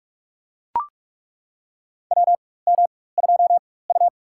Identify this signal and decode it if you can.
Morse code, heard as a keyed tone, about 42 wpm, spaced as for 20 wpm, 700 Hz. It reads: WM2U
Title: USA Call Signs - 4 Characters - 4x Character Spacing - 45wpm